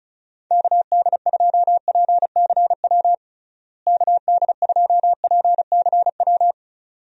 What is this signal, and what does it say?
Morse code, heard as a keyed tone, about 34 wpm, 700 Hz. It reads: KD2PCW KD2PCW
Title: USA Call Signs - 6 Characters - Repeated 2x - 35wpm